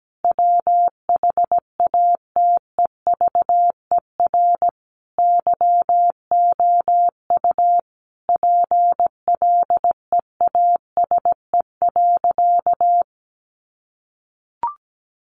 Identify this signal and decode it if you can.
Morse code, heard as a keyed tone, about 17 wpm, 700 Hz. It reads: WHATEVER YOU PLEASE.
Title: Sentences from Top 1500 Words 17wpm